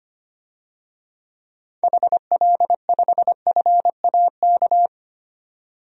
Morse code, heard as a keyed tone, about 25 wpm, 700 Hz. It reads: HL5FAK
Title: DXCC-100 Call Signs - 6 Characters - 25wpm